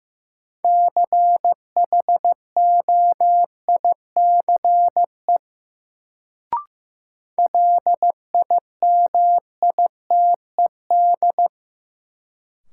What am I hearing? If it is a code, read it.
Morse code, heard as a keyed tone, about 16 wpm, 700 Hz. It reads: CHOICE LIMITED
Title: Sentences from Top 900 Words - New Word Drill - 15wpm